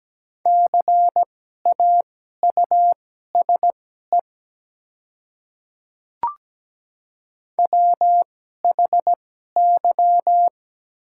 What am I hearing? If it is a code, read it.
Morse code, heard as a keyed tone, about 17 wpm, spaced as for 12 wpm, 700 Hz. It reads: CAUSE WHY
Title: 2x Character Spacing - Top 200 Words - 17wpm